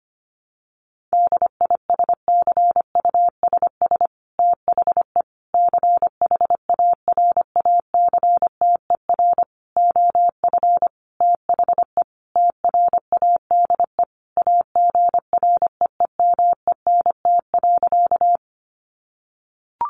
Morse code, heard as a keyed tone, about 25 wpm, 700 Hz. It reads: DISCUSS THE CHARACTER OF THE TRADE AGREEMENT.
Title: Sentences from Top 1500 Words 25wpm